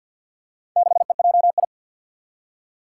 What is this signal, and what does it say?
Morse code, heard as a keyed tone, about 51 wpm, 700 Hz. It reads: 6EJI
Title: Four Challenging Letter-Number Combinations 50wpm